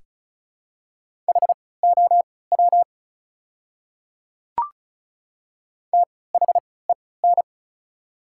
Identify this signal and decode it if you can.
Morse code, heard as a keyed tone, about 36 wpm, spaced as for 20 wpm, 700 Hz. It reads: HOW THEN
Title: 3x Character Spacing - Top 100 Words - 35wpm